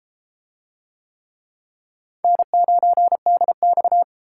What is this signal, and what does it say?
Morse code, heard as a keyed tone, about 30 wpm, 700 Hz. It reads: N9DX